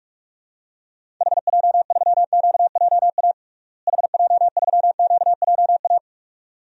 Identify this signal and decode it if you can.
Morse code, heard as a keyed tone, about 46 wpm, 700 Hz. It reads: HJ3QJA HJ3QJA
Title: DXCC-100 Call Signs - 6 Characters - Repeated 2x - 45wpm